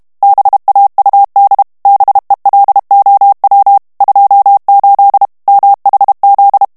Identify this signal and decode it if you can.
Morse code, heard as a keyed tone, about 32 wpm, 800 Hz. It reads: BAUD BELOW 28 MHZ
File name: ANGLAIS texte clair 3 vitesse 30 mots minute